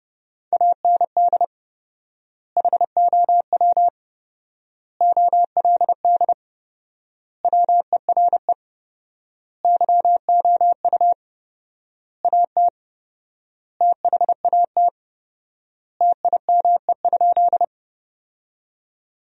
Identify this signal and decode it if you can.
Morse code, heard as a keyed tone, about 30 wpm, 700 Hz. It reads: AND HOW OLD WERE YOU AT THAT TIME?
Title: 4x Word-Spacing - Sentences from Top 200 Words - 30wpm